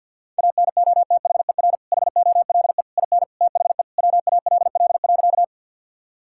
Morse code, heard as a keyed tone, about 50 wpm, 700 Hz. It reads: ANOTHER HOLE IN THE WALL.
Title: Sentences from Top 1600 Words 50wpm